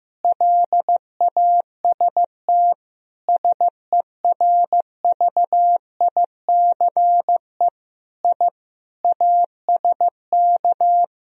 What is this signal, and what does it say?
Morse code, heard as a keyed tone, about 15 wpm, 700 Hz. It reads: LAST SERVICE I ASK
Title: Sentences from Top 1000 Words - Review Part 4 - 15wpm